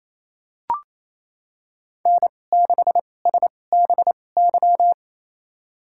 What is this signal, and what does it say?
Morse code, heard as a keyed tone, about 27 wpm, spaced as for 20 wpm, 700 Hz. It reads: N6SBY
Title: USA Call Signs - 2x Character Spacing - 28wpm